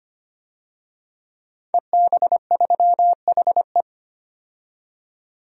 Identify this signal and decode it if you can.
Morse code, heard as a keyed tone, about 25 wpm, 700 Hz. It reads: EB3HE
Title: DXCC-100 Call Signs - 5 Characters - 25wpm